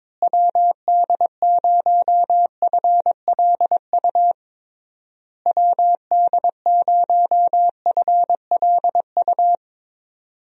Morse code, heard as a keyed tone, about 22 wpm, 700 Hz. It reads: WD0FLU WD0FLU
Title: USA Call Signs - 6 Characters - Repeated 3x - 22wpm